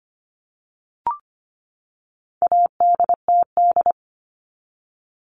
Morse code, heard as a keyed tone, about 25 wpm, 700 Hz. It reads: ADTD
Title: All Cut Numbers - 4-Digit 25wpm